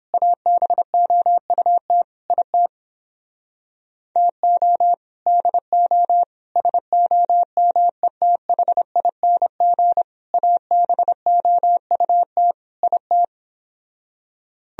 Morse code, heard as a keyed tone, about 29 wpm, 700 Hz. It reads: ABOUT IT TO DO SOMETHING ABOUT IT